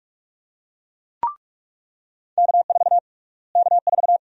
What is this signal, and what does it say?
Morse code, heard as a keyed tone, about 47 wpm, 700 Hz. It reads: K4 K4
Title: USA Call Sign Prefixes - Repeated 2x - 45wpm